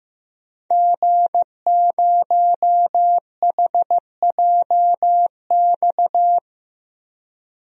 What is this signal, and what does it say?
Morse code, heard as a keyed tone, about 15 wpm, 700 Hz. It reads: G0HJX
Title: DXCC-100 Call Signs - 5 Characters - Repeated 2x - 15wpm